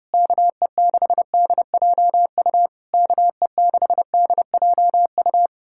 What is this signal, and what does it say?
Morse code, heard as a keyed tone, about 33 wpm, 700 Hz. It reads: KE6DJU KE6DJU